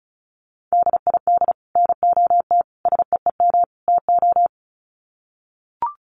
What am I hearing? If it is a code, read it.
Morse code, heard as a keyed tone, about 34 wpm, 700 Hz. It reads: DID NOT SEEM TO
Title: Sets of 4 Words 35wpm